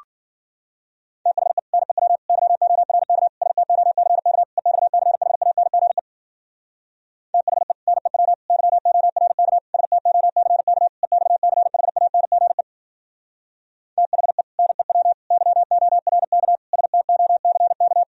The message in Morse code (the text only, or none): THE NEW YORK STOCK EXCHANGE THE NEW YORK STOCK EXCHANGE THE NEW YORK STOCK